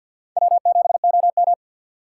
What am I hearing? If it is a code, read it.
Morse code, heard as a keyed tone, about 46 wpm, 700 Hz. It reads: W7OM
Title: USA Call Signs - 4 Characters - 6x Character Spacing - 50wpm